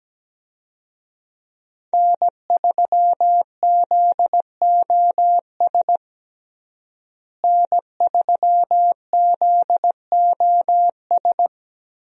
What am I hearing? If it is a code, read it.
Morse code, heard as a keyed tone, about 17 wpm, 700 Hz. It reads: N3ZOS N3ZOS
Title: USA Call Signs - 5 Characters - Repeated 2x - 17wpm